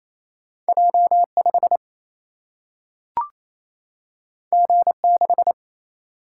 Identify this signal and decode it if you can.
Morse code, heard as a keyed tone, about 28 wpm, 700 Hz. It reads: J5 G6
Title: Single-Letter Single-Number 28wpm